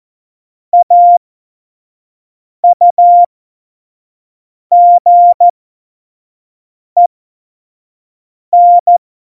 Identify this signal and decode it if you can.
Morse code, heard as a keyed tone, about 14 wpm, spaced as for 5 wpm, 700 Hz. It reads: AUGEN
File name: Morsekurs / Witze / Witz 28. Januar 2020, 25 F70